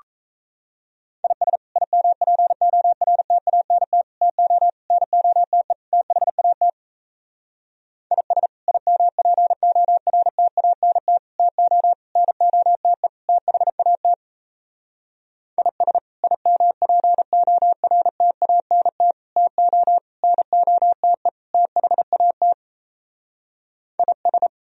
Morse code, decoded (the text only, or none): IS IMPORTANT TO NOTE THAT IS IMPORTANT TO NOTE THAT IS IMPORTANT TO NOTE THAT IS